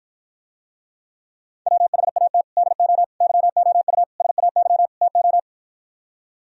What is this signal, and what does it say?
Morse code, heard as a keyed tone, about 59 wpm, 700 Hz. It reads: WHAT DO YOU SAY TO